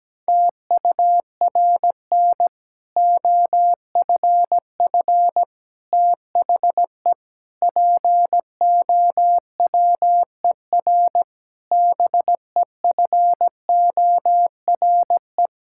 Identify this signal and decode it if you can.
Morse code, heard as a keyed tone, about 17 wpm, 700 Hz. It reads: TURN OFF THE POWER BEFORE
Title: Sentences from Top 1000 Words - Review Part 6 - 17wpm